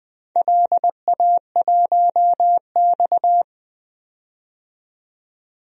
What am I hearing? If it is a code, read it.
Morse code, heard as a keyed tone, about 20 wpm, 700 Hz. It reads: LA1X
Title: DXCC-100 Call Signs - 4 Characters - 20wpm